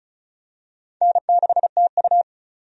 Morse code, heard as a keyed tone, about 34 wpm, 700 Hz. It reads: N6TU
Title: USA Call Signs - 4 Characters - 35wpm